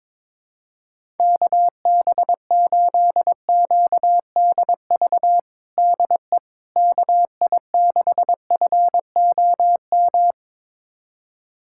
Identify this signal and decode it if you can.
Morse code, heard as a keyed tone, about 22 wpm, 700 Hz. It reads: KB8QDV DE KI6FOM
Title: USA Callsign DE Callsign - 6 Characters - 22wpm